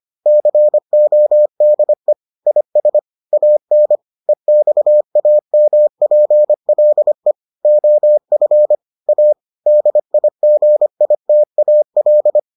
Morse code, heard as a keyed tone, about 25 wpm, 600 Hz. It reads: CODE IS AN EXAMPLE OF A DIGITAL